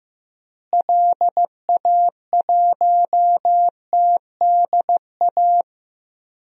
Morse code, heard as a keyed tone, about 15 wpm, 700 Hz. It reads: LA1TDA
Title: DXCC-100 Call Signs - 6 Characters - Repeated 2x - 15wpm